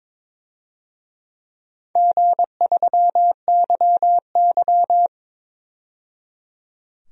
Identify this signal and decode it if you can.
Morse code, heard as a keyed tone, about 22 wpm, 700 Hz. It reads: G3YY
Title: DXCC-100 Call Signs - 4 Characters - 22wpm